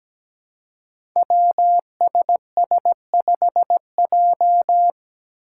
Morse code, heard as a keyed tone, about 17 wpm, 700 Hz. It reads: WSS5J
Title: Five Challenging Letter-Number Combinations 17wpm